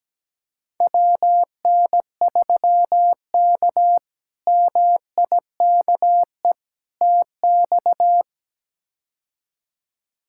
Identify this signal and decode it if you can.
Morse code, heard as a keyed tone, about 17 wpm, 700 Hz. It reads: WN3K MIKE TX
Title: K1USN SST QSO Practice - 17wpm